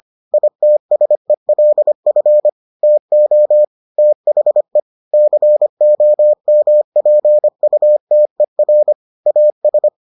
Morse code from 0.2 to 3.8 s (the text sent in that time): ITSELF TO